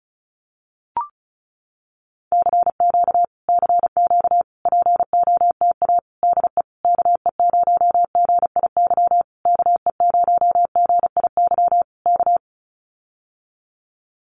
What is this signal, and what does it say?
Morse code, heard as a keyed tone, about 34 wpm, 700 Hz. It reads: CQ CQ POTA DE KE0GIY KE0GIY K
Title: Common QSO Phrases Encore 35wpm